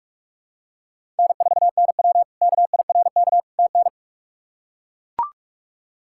Morse code, heard as a keyed tone, about 41 wpm, 700 Hz. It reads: N4NW KIRK TN
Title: K1USN SST QSO Practice - 45wpm